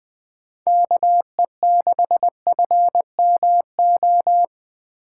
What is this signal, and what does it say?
Morse code, heard as a keyed tone, about 21 wpm, 700 Hz. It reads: KE6FMO